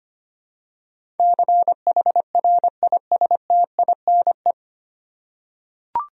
Morse code, heard as a keyed tone, about 25 wpm, 700 Hz. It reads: CHRISTINE
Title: USA Common Names - 25wpm